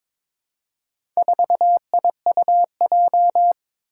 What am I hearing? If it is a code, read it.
Morse code, heard as a keyed tone, about 23 wpm, 700 Hz. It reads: 4IUJ
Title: Four Challenging Letter-Number Combinations 22wpm